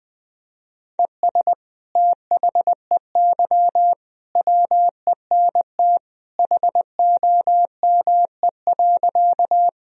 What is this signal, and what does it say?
Morse code, heard as a keyed tone, about 21 wpm, 700 Hz. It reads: ES THEY WENT HOME.
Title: Sentences from Top 200 Words and Abbreviations 20wpm